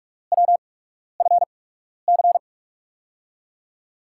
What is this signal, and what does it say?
Morse code, heard as a keyed tone, about 43 wpm, spaced as for 12 wpm, 700 Hz. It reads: WFC
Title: USA Call Signs - 5 Characters - 8x Character Spacing - 45wpm